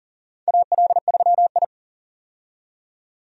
Morse code, heard as a keyed tone, about 40 wpm, 700 Hz. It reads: AL3I
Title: USA Call Signs - 4 Characters - 40wpm